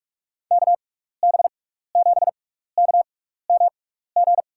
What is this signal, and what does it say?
Morse code, heard as a keyed tone, about 46 wpm, spaced as for 15 wpm, 700 Hz. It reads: KB7KMG
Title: USA Call Signs - 6 Characters - 6x Character Spacing - 45wpm